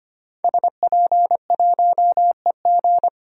Morse code, heard as a keyed tone, about 25 wpm, 700 Hz. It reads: SP1EG
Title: DXCC-100 Call Signs - 5 Characters - 25wpm